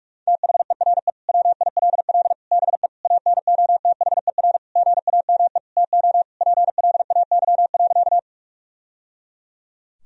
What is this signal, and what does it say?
Morse code, heard as a keyed tone, about 45 wpm, 700 Hz. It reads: THERE WILL BE ANOTHER GAME TO PLAY.
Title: Sentences from Top 1000 Words - Review Part 3 - 45wpm